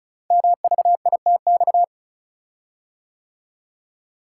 Morse code, heard as a keyed tone, about 36 wpm, 700 Hz. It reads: MVITX
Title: Code Groups - 5 Letters - 35wpm